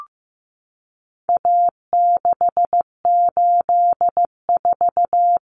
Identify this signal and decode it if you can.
Morse code, heard as a keyed tone, about 15 wpm, 700 Hz. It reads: A684
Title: Common Cut Numbers - 4-Digit 15wpm